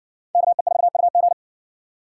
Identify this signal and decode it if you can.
Morse code, heard as a keyed tone, about 58 wpm, 700 Hz. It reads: KE4UG